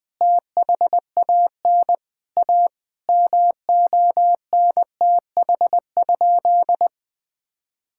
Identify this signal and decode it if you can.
Morse code, heard as a keyed tone, about 21 wpm, 700 Hz. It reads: THAN A MONTH?
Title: Sentences from Top 1000 Words - Review Part 6 - 20wpm